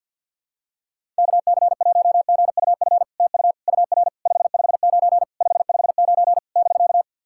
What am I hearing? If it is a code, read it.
Morse code, heard as a keyed tone, about 50 wpm, 700 Hz. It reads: KC1GUF TU UR 559 559 <BK>
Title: SOTA QSO Practice 50wpm